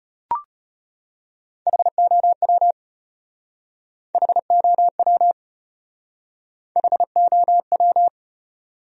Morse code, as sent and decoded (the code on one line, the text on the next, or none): .... --- .-- / .... --- .-- / .... --- .--
HOW HOW HOW